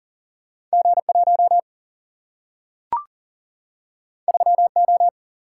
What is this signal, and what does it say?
Morse code, heard as a keyed tone, about 40 wpm, 700 Hz. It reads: G1 3O
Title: Single-Letter Single-Number 40wpm